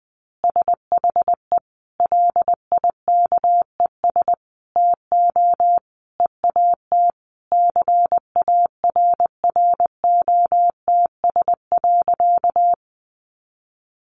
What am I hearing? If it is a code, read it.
Morse code, heard as a keyed tone, about 20 wpm, 700 Hz. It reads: SHE LIKES TO EAT CARROTS.
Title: Easy Phrases 20wpm